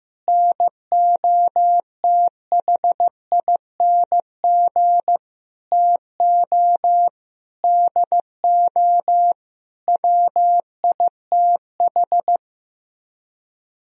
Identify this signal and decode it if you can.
Morse code, heard as a keyed tone, about 15 wpm, 700 Hz. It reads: NOTHING TO DO WITH
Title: Speed-Racing - Sets of 5 Words 15wpm